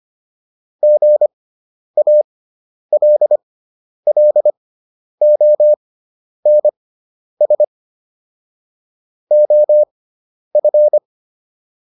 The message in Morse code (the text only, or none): GALLONS OF